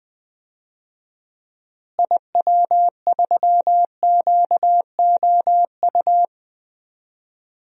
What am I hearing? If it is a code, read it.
Morse code, heard as a keyed tone, about 20 wpm, 700 Hz. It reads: IW3QOU
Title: DXCC-100 Call Signs - 6 Characters - Repeated 3x - 20wpm